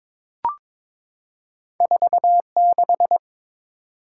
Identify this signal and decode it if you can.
Morse code, heard as a keyed tone, about 21 wpm, 700 Hz. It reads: 46
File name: practice / 22wpm / Two Letter-Number Combinations - 22wpm